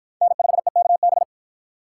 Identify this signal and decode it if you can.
Morse code, heard as a keyed tone, about 53 wpm, 700 Hz. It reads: N5EKB